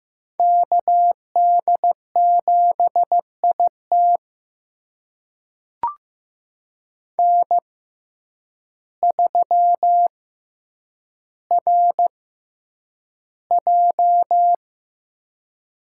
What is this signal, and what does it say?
Morse code, heard as a keyed tone, about 15 wpm, spaced as for 8 wpm, 700 Hz. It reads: KD7IT N3RJ
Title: USA Call Signs - 5 Characters - 6x Character Spacing - 15wpm